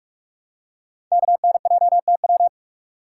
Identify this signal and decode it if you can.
Morse code, heard as a keyed tone, about 44 wpm, 700 Hz. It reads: KNJTW
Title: Code Groups - 5 Letter:Numbers - 45wpm